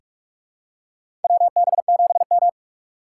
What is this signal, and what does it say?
Morse code, heard as a keyed tone, about 46 wpm, 700 Hz. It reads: WB7M
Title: USA Call Signs - 3x Character Spacing - 45wpm